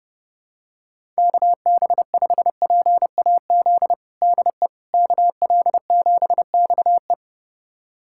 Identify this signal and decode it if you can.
Morse code, heard as a keyed tone, about 31 wpm, 700 Hz. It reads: KB5PAZ DE KL7XE
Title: USA Callsign DE Callsign 30wpm